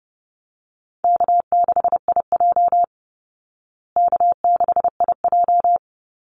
Morse code, heard as a keyed tone, about 30 wpm, 700 Hz. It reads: K6IJ K6IJ
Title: Easy Call Signs Repeated 2x - 30wpm